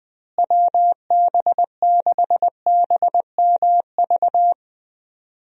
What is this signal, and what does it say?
Morse code, heard as a keyed tone, about 20 wpm, 700 Hz. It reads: WB6BMV
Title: USA Call Signs - 6 Characters - Repeated 2x - 20wpm